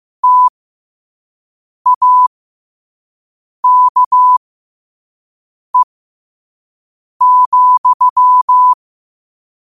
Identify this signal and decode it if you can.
Morse code, heard as a keyed tone, about 15 wpm, spaced as for 5 wpm, 1000 Hz. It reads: TAKE,
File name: qotdfiles / 5wpm